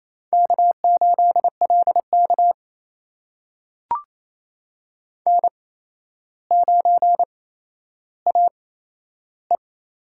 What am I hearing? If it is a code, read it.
Morse code, heard as a keyed tone, about 29 wpm, spaced as for 12 wpm, 700 Hz. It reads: K8LK N9AE
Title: USA Call Signs - 4 Characters - 8x Character Spacing - 28wpm